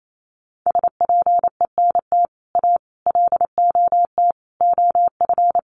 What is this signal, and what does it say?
Morse code, heard as a keyed tone, about 28 wpm, 700 Hz. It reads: SPENT A LOT OF
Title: Sets of 5 Words 28wpm